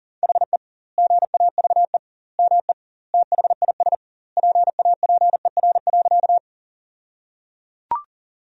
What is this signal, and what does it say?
Morse code, heard as a keyed tone, about 37 wpm, 700 Hz. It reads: HE GAVE ME THIS PAPER.
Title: 2x Word-Spacing - Sentences from Top 500 Words - 40wpm